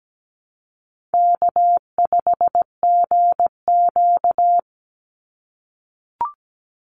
Morse code, heard as a keyed tone, about 17 wpm, 700 Hz. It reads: K5GQ
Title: Easy Call Signs - 17wpm